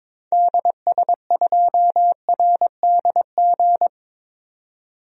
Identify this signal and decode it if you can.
Morse code, heard as a keyed tone, about 22 wpm, 700 Hz. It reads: DS2RDG
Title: DXCC-100 Call Signs - 6 Characters - 22wpm